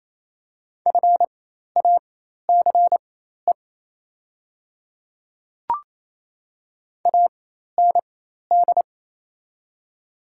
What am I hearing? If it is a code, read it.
Morse code, heard as a keyed tone, about 28 wpm, spaced as for 13 wpm, 700 Hz. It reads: FACE AND
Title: 4x Character Spacing - Top 300 Words - 28wpm